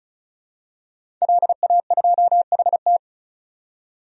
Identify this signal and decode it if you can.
Morse code, heard as a keyed tone, about 34 wpm, 700 Hz. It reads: LA2HT
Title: DXCC-100 Call Signs - 5 Characters - 35wpm